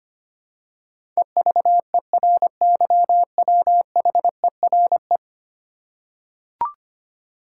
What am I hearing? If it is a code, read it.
Morse code, heard as a keyed tone, about 25 wpm, 700 Hz. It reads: EVERYWHERE
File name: 10 Letter Words - First 3 Letters Given - 25wpm